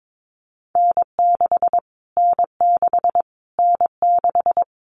Morse code, heard as a keyed tone, about 22 wpm, 700 Hz. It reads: N6 N6 N6
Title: Callsign Prefixes for Top 25 DXCC - Repeated 3x - 22wpm